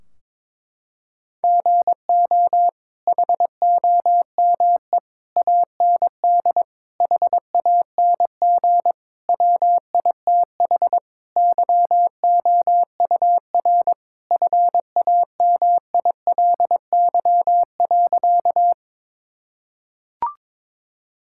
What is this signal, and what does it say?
Morse code, heard as a keyed tone, about 22 wpm, 700 Hz. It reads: GO HOME AND HANG WITH YOUR FAMILY.
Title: Sentences from Top 1600 Words 22wpm